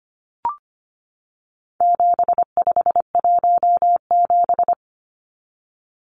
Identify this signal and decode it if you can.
Morse code, heard as a keyed tone, about 25 wpm, 700 Hz. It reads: 7517
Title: Numbers - 4-Digit 25wpm